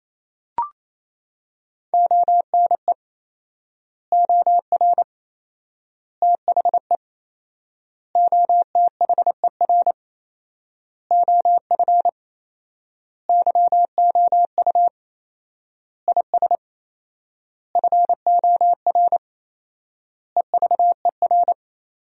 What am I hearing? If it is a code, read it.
Morse code, heard as a keyed tone, about 28 wpm, 700 Hz. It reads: ONE OR THE OTHER OF YOU IS FOR EVER
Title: 4x Word-Spacing - Sentences from Top 400 Words - 28wpm